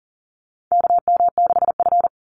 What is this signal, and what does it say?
Morse code, heard as a keyed tone, about 40 wpm, 700 Hz. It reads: KM6F
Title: USA Call Signs 40wpm